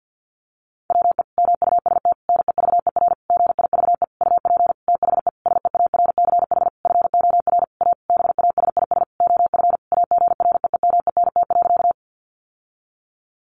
Morse code, heard as a keyed tone, about 50 wpm, 700 Hz. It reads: WE MUST NEVER GIVE UP THE SEARCH FOR A BASIS OF AGREEMENT.
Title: Sentences from Top 1500 Words 50wpm